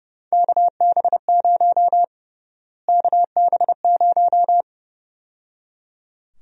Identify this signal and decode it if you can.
Morse code, heard as a keyed tone, about 30 wpm, 700 Hz. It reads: KB0 KB0
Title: USA Call Sign Prefixes - Repeated 2x - 30wpm